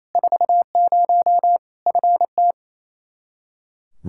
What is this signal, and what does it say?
Morse code, heard as a keyed tone, about 28 wpm, 700 Hz. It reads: 40 FT
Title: K1USN SST QSO Practice 28wpm